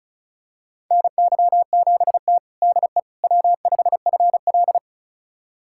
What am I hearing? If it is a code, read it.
Morse code, heard as a keyed tone, about 36 wpm, 700 Hz. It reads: NY7T DE W5FL
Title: USA Callsign DE Callsign - 4 Characters - 35wpm